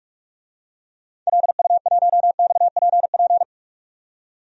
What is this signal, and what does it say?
Morse code, heard as a keyed tone, about 47 wpm, 700 Hz. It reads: LU1XPP
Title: DXCC-100 Call Signs - 6 Characters - 45wpm